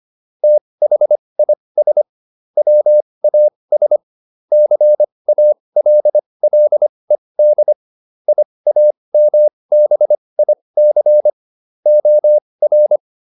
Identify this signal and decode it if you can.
Morse code, heard as a keyed tone, about 25 wpm, spaced as for 20 wpm, 600 Hz. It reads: THIS WAS CALLED IAMBIC OR
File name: SS315 20